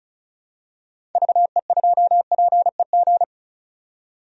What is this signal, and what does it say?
Morse code, heard as a keyed tone, about 36 wpm, 700 Hz. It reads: VE2PEG
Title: DXCC-100 Call Signs - 6 Characters - 35wpm